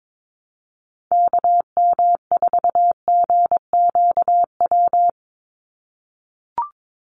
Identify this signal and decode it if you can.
Morse code, heard as a keyed tone, about 22 wpm, 700 Hz. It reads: KM4GQW
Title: USA Call Signs 22wpm